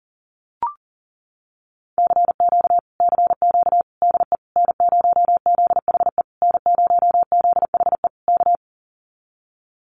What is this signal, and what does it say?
Morse code, heard as a keyed tone, about 40 wpm, 700 Hz. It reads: CQ CQ DE N0ZHE N0ZHE K
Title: Common QSO Phrases Encore 40wpm